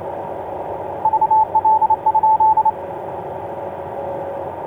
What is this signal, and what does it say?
Morse code, heard as a keyed tone, about 28 wpm, 870 Hz, with radio noise. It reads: VL?
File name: ox-dl2vfr i4mny etc 12m cw 22oct2010 1613